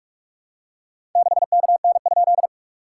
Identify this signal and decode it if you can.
Morse code, heard as a keyed tone, about 46 wpm, 690 Hz. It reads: 6KN?
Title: SOTA QSO Practice 45wpm